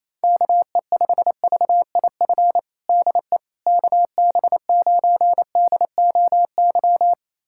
Morse code, heard as a keyed tone, about 27 wpm, 700 Hz. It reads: KE5VIF DE KB9DOY
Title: USA Callsign DE Callsign - 6 Characters - 28wpm